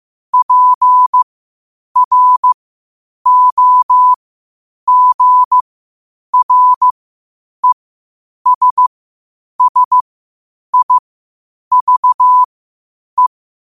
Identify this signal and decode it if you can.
Morse code, heard as a keyed tone, about 15 wpm, spaced as for 8 wpm, 1000 Hz. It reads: PROGRESSIVE
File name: qotdfiles / 8wpm